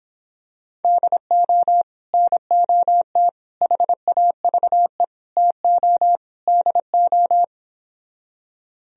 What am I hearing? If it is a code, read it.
Morse code, heard as a keyed tone, about 26 wpm, 700 Hz. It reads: DO NOT HAVE TO DO